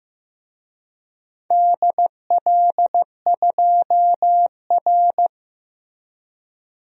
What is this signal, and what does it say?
Morse code, heard as a keyed tone, about 15 wpm, 700 Hz. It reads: DL2R
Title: DXCC-100 Call Signs - 4 Characters - Repeated 3x - 15wpm